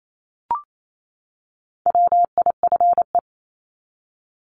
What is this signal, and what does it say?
Morse code, heard as a keyed tone, about 28 wpm, 700 Hz. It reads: WIFE
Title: Top 1751-2000 Words 28wpm